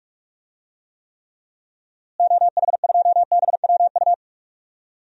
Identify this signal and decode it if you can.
Morse code, heard as a keyed tone, about 46 wpm, 700 Hz. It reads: OH2BWU
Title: DXCC-100 Call Signs - 6 Characters - Repeated 2x - 45wpm